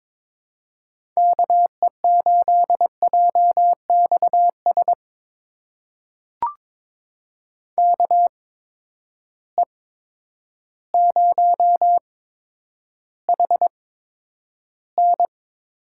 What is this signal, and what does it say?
Morse code, heard as a keyed tone, about 22 wpm, spaced as for 10 wpm, 700 Hz. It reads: KE8JXS KE0HN
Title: USA Call Signs - 6 Characters - 8x Character Spacing - 22wpm